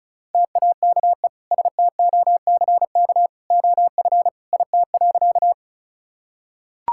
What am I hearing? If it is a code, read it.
Morse code, heard as a keyed tone, about 36 wpm, 700 Hz. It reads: TAKE STOCK OF IT.
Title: Sentences from Top 1000 Words - Review Part 5 - 35wpm